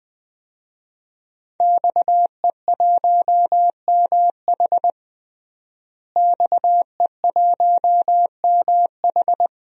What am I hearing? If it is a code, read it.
Morse code, heard as a keyed tone, about 21 wpm, 700 Hz. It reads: XE1MH XE1MH
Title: DXCC-100 Call Signs - 5 Characters - Repeated 3x - 20wpm